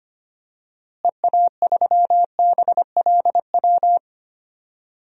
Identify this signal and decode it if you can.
Morse code, heard as a keyed tone, about 25 wpm, 700 Hz. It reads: EA3BLW
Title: DXCC-100 Call Signs - 6 Characters - Repeated 3x - 25wpm